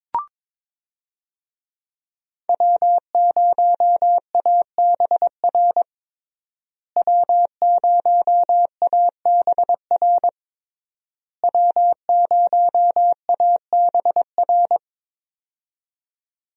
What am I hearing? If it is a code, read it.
Morse code, heard as a keyed tone, about 22 wpm, 690 Hz. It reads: W0ABR W0ABR W0ABR
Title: USA Call Signs - 5 Characters - Repeated 3x - 22wpm